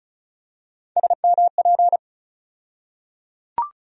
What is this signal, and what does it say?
Morse code, heard as a keyed tone, about 34 wpm, 700 Hz. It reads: SMP
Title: USA Call Sign Postfixes - 3 Letters 35wpm